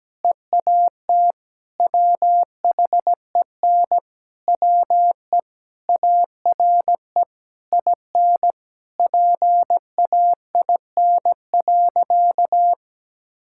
Morse code, heard as a keyed tone, about 17 wpm, 700 Hz. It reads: EAT WHEN WE ARE IN PAIN.
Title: Sentences from Top 1600 Words 17wpm